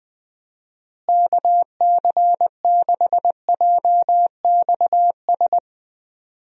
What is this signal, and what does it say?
Morse code, heard as a keyed tone, about 20 wpm, 700 Hz. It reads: KC6JXS
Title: USA Call Signs - 6 Characters - Repeated 2x - 20wpm